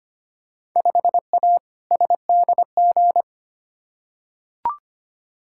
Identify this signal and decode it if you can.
Morse code, heard as a keyed tone, about 25 wpm, 700 Hz. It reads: 5A SDG
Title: Field Day Exchanges Encore - 25wpm